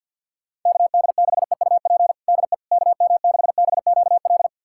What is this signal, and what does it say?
Morse code, heard as a keyed tone, about 46 wpm, 690 Hz. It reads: KD6EUP DE KM6BQL
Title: USA Callsign DE Callsign - 6 Characters - 50wpm